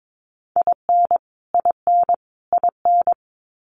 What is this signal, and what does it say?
Morse code, heard as a keyed tone, about 22 wpm, 700 Hz. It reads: IN IN IN
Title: Top 300 Words Repeated 22wpm